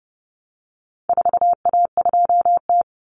Morse code, heard as a keyed tone, about 30 wpm, 700 Hz. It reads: 4A2T
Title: Common Cut Numbers - 4-Digit 30wpm